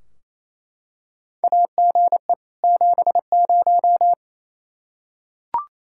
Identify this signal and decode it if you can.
Morse code, heard as a keyed tone, about 28 wpm, 700 Hz. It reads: AGE 70
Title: Practice QSO - Exchange 3 - More Info - 28wpm